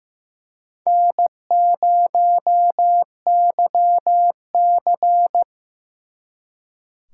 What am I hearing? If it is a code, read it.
Morse code, heard as a keyed tone, about 18 wpm, 700 Hz. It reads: N0YC
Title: Speed Racing - USA 1x2 Call Signs 15wpm